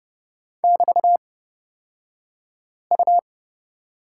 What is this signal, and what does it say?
Morse code, heard as a keyed tone, about 31 wpm, spaced as for 5 wpm, 700 Hz. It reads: <BT>U
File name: Lesson 36d - BT - Two Characters 30wpm